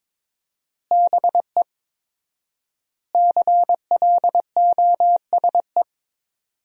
Morse code, heard as a keyed tone, about 21 wpm, 700 Hz. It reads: BE CLOSE
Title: 4x Word-Spacing - Sentences from Top 300 Words - 22wpm